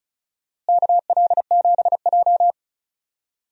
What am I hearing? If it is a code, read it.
Morse code, heard as a keyed tone, about 34 wpm, 700 Hz. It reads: KL7J
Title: DXCC-100 Call Signs - 4 Characters - Repeated 3x - 35wpm